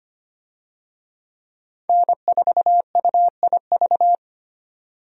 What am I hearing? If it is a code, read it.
Morse code, heard as a keyed tone, about 25 wpm, 700 Hz. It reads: N4UIV
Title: USA Call Signs - 5 Characters - 25wpm